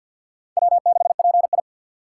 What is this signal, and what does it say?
Morse code, heard as a keyed tone, about 51 wpm, 700 Hz. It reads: W6PI
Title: USA Call Signs - 3x Character Spacing - 50wpm